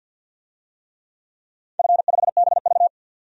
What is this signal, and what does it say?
Morse code, heard as a keyed tone, about 45 wpm, 700 Hz. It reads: F5BV
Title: DXCC-100 Call Signs - 4 Characters - 50wpm